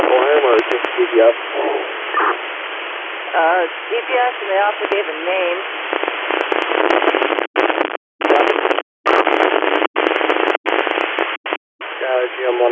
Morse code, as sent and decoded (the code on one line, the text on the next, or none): none